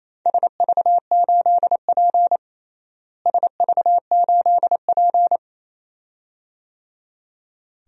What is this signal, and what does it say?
Morse code, heard as a keyed tone, about 28 wpm, 700 Hz. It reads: SV8P SV8P
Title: DXCC-100 Call Signs - 4 Characters - Repeated 2x - 28wpm